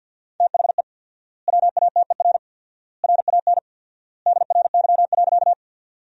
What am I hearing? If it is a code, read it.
Morse code, heard as a keyed tone, about 50 wpm, 700 Hz. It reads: THE WATER RAN DRY.
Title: 4x Word-Spacing - Sentences from Top 500 Words - 50wpm